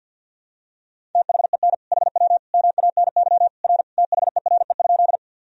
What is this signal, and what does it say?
Morse code, heard as a keyed tone, about 52 wpm, 700 Hz. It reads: THEN HW MANY R THERE?
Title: Sentences from Top 100 Words and Abbreviations 50wpm